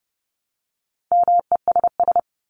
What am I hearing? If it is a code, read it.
Morse code, heard as a keyed tone, about 30 wpm, 700 Hz. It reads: MESS
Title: Top 1751-2000 Words 30wpm